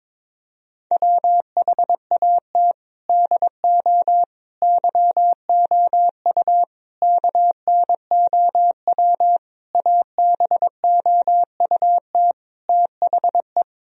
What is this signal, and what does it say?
Morse code, heard as a keyed tone, about 22 wpm, 700 Hz. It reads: WHAT DO YOU KNOW ABOUT THE
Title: Sentences from Top 1000 Words - Review Part 4 - 22wpm